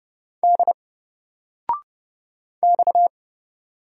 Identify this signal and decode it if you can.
Morse code, heard as a keyed tone, about 30 wpm, 700 Hz. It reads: D X
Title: Single Letter-Number 30wpm